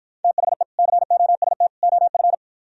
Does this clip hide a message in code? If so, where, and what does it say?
Yes: Morse code from 0.2 to 2.3 s, heard as a keyed tone, about 52 wpm, 700 Hz. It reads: THE COST OF